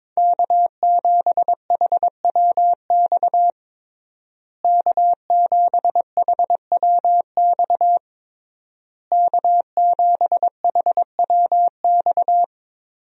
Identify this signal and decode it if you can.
Morse code, heard as a keyed tone, about 22 wpm, 700 Hz. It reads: K7HWX K7HWX K7HWX
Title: USA Call Signs - 5 Characters - Repeated 3x - 22wpm